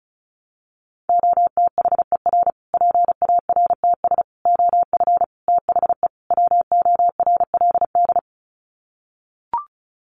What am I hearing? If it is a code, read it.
Morse code, heard as a keyed tone, about 34 wpm, 700 Hz. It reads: OTHER PARTS OF THE WORLD
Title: Sets of 5 Words Encore 35wpm